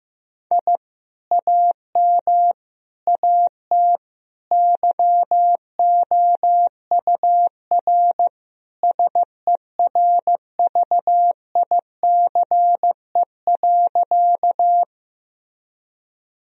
Morse code, heard as a keyed tone, about 15 wpm, 700 Hz. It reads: I AM AT YOUR SERVICE.
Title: Sentences from Top 1000 Words - Review Part 4 - 15wpm